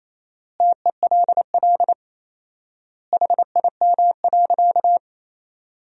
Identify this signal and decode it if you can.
Morse code, heard as a keyed tone, about 28 wpm, 700 Hz. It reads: TELL HIM.
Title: 4x Word-Spacing - Sentences from Top 200 Words - 28wpm